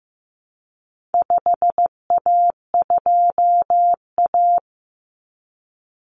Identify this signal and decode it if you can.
Morse code, heard as a keyed tone, about 15 wpm, 700 Hz. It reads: 5A2A
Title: Common Cut Numbers - 4-Digit 15wpm